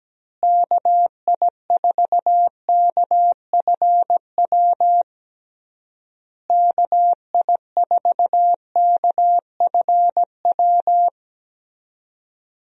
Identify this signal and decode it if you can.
Morse code, heard as a keyed tone, about 17 wpm, 700 Hz. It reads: KI4KFW KI4KFW
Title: USA Call Signs - 6 Characters - Repeated 3x - 17wpm